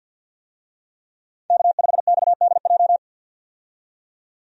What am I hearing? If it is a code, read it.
Morse code, heard as a keyed tone, about 51 wpm, 700 Hz. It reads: K5XDJ
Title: USA Call Signs - 5 Characters - 50wpm